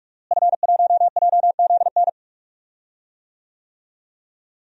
Morse code, heard as a keyed tone, about 46 wpm, 700 Hz. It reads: F1JZN
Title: DXCC-100 Call Signs - 5 Characters - Repeated 2x - 45wpm